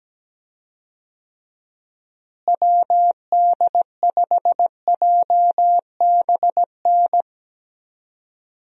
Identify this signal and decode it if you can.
Morse code, heard as a keyed tone, about 17 wpm, 700 Hz. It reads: WD5JBN
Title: USA Call Signs - 6 Characters - Repeated 2x - 17wpm